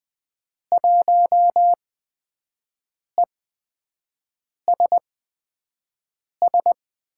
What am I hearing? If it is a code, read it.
Morse code, heard as a keyed tone, about 20 wpm, spaced as for 5 wpm, 700 Hz. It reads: 1ESS
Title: USA Call Signs - 6 Characters - 8x Character Spacing - 20wpm